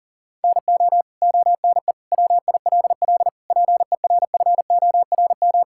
Morse code, heard as a keyed tone, about 40 wpm, 700 Hz. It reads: NO ONE WILL PERFORM
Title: Sentences from Top 1000 Words - Review Part 5 - 40wpm